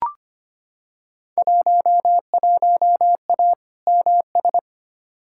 Morse code, heard as a keyed tone, about 25 wpm, 700 Hz. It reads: 11A MS
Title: Field Day Exchanges Encore - 25wpm